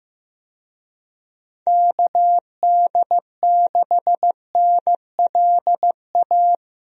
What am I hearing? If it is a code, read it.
Morse code, heard as a keyed tone, about 15 wpm, 700 Hz. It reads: KD6NLA
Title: USA Call Signs - 6 Characters - Repeated 2x - 15wpm